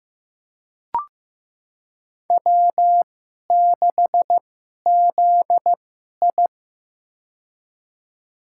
Morse code, heard as a keyed tone, about 15 wpm, spaced as for 11 wpm, 700 Hz. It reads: W6ZI
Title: USA Call Signs - 4 Characters - 2x Character Spacing - 15wpm